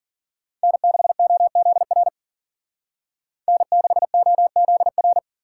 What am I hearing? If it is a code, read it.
Morse code, heard as a keyed tone, about 48 wpm, 700 Hz. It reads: N6OZR N6OZR
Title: Speed Racing - USA 1x3 Call Signs 40wpm